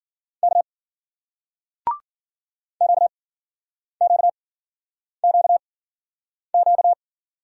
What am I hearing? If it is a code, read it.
Morse code, heard as a keyed tone, about 50 wpm, spaced as for 9 wpm, 700 Hz. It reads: K QQQQ
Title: Speed-Racing Single Letters 40wpm